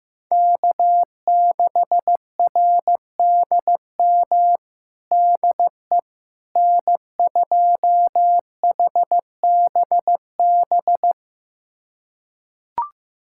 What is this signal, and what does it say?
Morse code, heard as a keyed tone, about 15 wpm, 700 Hz. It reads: K6RDM DE N2HBB
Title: USA Callsign DE Callsign - 5 Characters - 15wpm